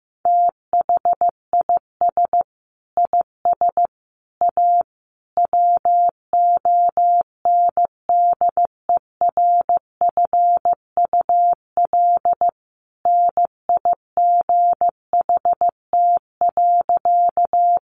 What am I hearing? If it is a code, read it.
Morse code, heard as a keyed tone, about 15 wpm, 700 Hz. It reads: THIS IS A WONDERFUL NIGHT.
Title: Sentences from Top 1100 Words 15wpm